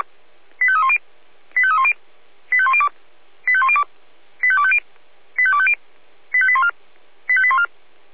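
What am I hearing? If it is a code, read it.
Five-tone selective call (ZVEI), heard as quick runs of five tones, tones of about 70 ms. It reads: 85310 85310 85102 85102 85250 85250 85814 85814
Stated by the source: ZVEI five-tone call received on SDR.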